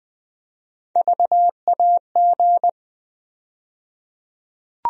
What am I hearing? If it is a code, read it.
Morse code, heard as a keyed tone, about 20 wpm, 700 Hz. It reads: VAG
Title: USA Call Sign Postfixes - 3 Letters 20wpm